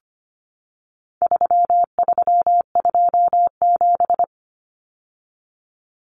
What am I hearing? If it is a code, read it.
Morse code, heard as a keyed tone, about 25 wpm, 700 Hz. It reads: 3327
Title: Common Cut Numbers - 4-Digit 25wpm